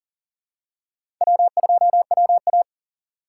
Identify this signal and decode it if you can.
Morse code, heard as a keyed tone, about 40 wpm, 700 Hz. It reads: W2WA
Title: USA Call Signs - 4 Characters - 8x Character Spacing - 40wpm